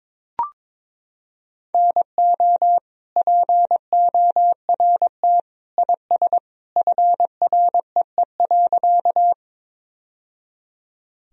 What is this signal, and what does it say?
Morse code, heard as a keyed tone, about 22 wpm, 700 Hz. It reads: NO PORT IS FREE.
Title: Sentences from Top 1000 Words - Review Part 1 - 22wpm